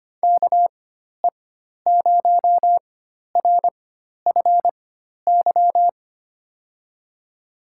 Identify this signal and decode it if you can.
Morse code, heard as a keyed tone, about 25 wpm, spaced as for 12 wpm, 700 Hz. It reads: KE0RFY
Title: USA Call Signs - 6 Characters - 4x Character Spacing - 25wpm